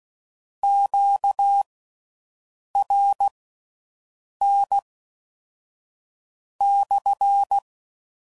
Morse code, heard as a keyed tone, about 16 wpm, spaced as for 5 wpm, 800 Hz. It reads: QRN/